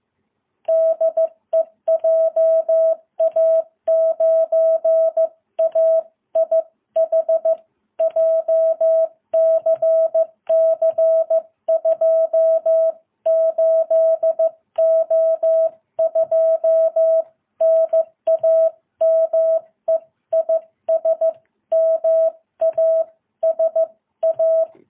Morse code, heard as a keyed tone, about 15 wpm, 650 Hz. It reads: DE JA9AIH JCC 28O2 NAME IS MASA
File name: jh8cbh